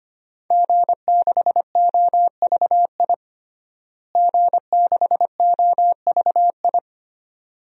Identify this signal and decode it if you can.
Morse code, heard as a keyed tone, about 25 wpm, 700 Hz. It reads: G6OVI G6OVI
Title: DXCC-100 Call Signs - 5 Characters - Repeated 2x - 25wpm